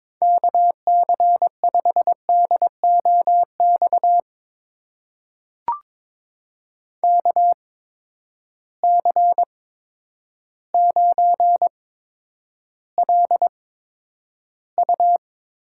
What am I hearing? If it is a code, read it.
Morse code, heard as a keyed tone, about 22 wpm, spaced as for 10 wpm, 700 Hz. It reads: KC5DOX KC9LU
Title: USA Call Signs - 6 Characters - 8x Character Spacing - 22wpm